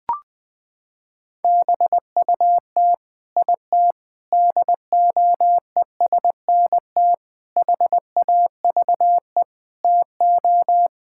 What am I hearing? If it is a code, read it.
Morse code, heard as a keyed tone, about 21 wpm, 700 Hz. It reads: BUT IT DOESNT HAVE TO
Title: Sets of 8 Words - 20wpm